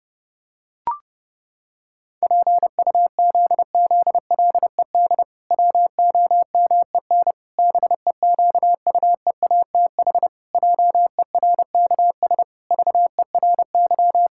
Morse code, heard as a keyed tone, about 32 wpm, 700 Hz. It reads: PUZZLED WOMEN BEQUEATH JERKS VERY
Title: Pangram Challenge v2 - 30wpm